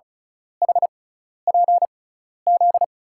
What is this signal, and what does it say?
Morse code, heard as a keyed tone, about 36 wpm, spaced as for 12 wpm, 700 Hz. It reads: HPZ
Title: USA Call Signs - 6 Characters - 6x Character Spacing - 35wpm